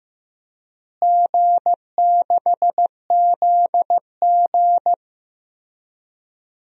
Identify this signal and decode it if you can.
Morse code, heard as a keyed tone, about 15 wpm, 700 Hz. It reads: G6ZG
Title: DXCC-100 Call Signs - 4 Characters - Repeated 3x - 15wpm